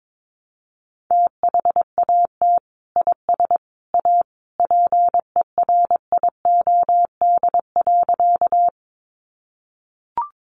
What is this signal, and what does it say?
Morse code, heard as a keyed tone, about 22 wpm, 700 Hz. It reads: THAT IS A PERIOD.
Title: Sentences from Top 800 Words 22wpm